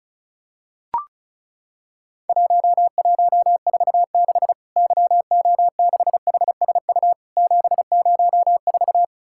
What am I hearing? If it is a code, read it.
Morse code, heard as a keyed tone, about 36 wpm, 700 Hz. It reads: 1146 YO6HSU 704
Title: Worked All Europe DX Contest - QTC Exchange - 35wpm